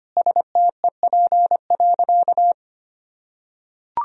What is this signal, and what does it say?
Morse code, heard as a keyed tone, about 25 wpm, 700 Hz. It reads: STEP.
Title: Sentences from Top 1000 Words - Review Part 2 - 25wpm